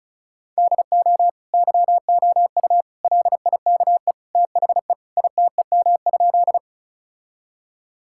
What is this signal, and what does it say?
Morse code, heard as a keyed tone, about 36 wpm, 700 Hz. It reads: DO YOU LIKE THE ITEM?
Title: Sentences from Top 1000 Words - Review Part 6 - 35wpm